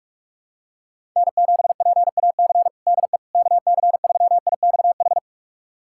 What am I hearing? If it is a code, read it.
Morse code, heard as a keyed tone, about 42 wpm, 700 Hz. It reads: N7PAC DE KC3IXH
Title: USA Callsign DE Callsign 45wpm